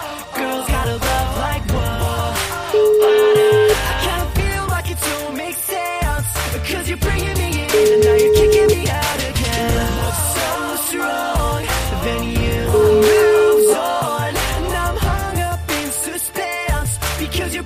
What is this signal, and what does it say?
Ringback tone (Europe, 425 Hz), with noise.